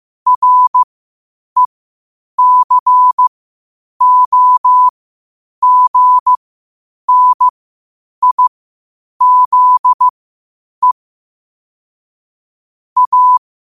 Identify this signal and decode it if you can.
Morse code, heard as a keyed tone, about 15 wpm, spaced as for 8 wpm, 1000 Hz. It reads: RECOGNIZE A
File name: 8wpm